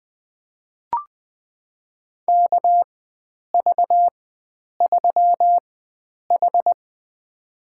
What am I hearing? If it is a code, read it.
Morse code, heard as a keyed tone, about 20 wpm, spaced as for 9 wpm, 700 Hz. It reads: KV3H
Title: USA Call Signs - 4 Characters - 4x Character Spacing - 20wpm